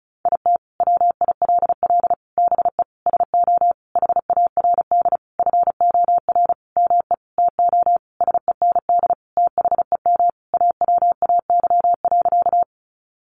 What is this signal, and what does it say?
Morse code, heard as a keyed tone, about 34 wpm, 700 Hz. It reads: IT WILL BE SO HARD FOR ME TO SEND THEM AWAY.
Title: Sentences from Top 1500 Words 35wpm